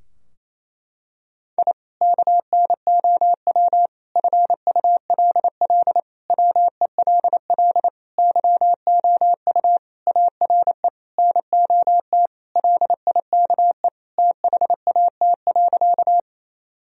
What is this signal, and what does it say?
Morse code, heard as a keyed tone, about 28 wpm, 700 Hz. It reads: I KNOW FULL WELL YOU ARE NOT LIKE THAT.
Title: Sentences from Top 1000 Words - Review Part 3 - 28wpm